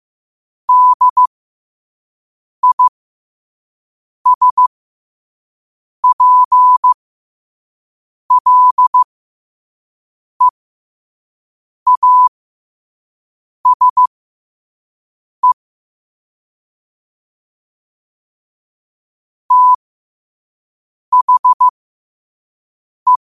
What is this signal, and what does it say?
Morse code, heard as a keyed tone, about 15 wpm, spaced as for 5 wpm, 1000 Hz. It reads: DISPLEASE THE
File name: qotdfiles / 5wpm